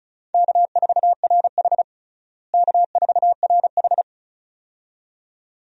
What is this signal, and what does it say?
Morse code, heard as a keyed tone, about 36 wpm, 700 Hz. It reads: K4RH K4RH
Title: USA Call Signs - 4 Characters - Repeated 2x - 35wpm